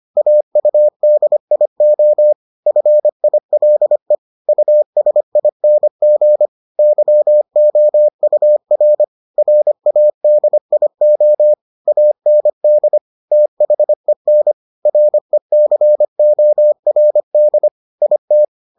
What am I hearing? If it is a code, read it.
Morse code, heard as a keyed tone, about 25 wpm, 600 Hz. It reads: AUDIO FILE USING YOUR RADIO AND THEN RECORD IT